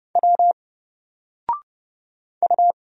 Morse code, heard as a keyed tone, about 30 wpm, 700 Hz. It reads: W U
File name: Single-Letter-Number-30wpm